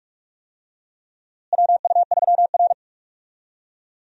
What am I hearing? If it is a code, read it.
Morse code, heard as a keyed tone, about 42 wpm, 700 Hz. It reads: WU3R
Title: USA Call Signs - 4 Characters - Repeated 2x - 45wpm